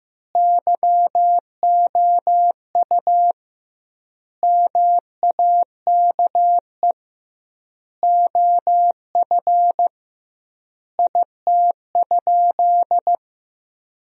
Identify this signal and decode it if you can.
Morse code, heard as a keyed tone, about 15 wpm, 700 Hz. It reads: YOU MAKE OF IT?
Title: 2x Word-Spacing - Sentences from Top 100 Words - 15wpm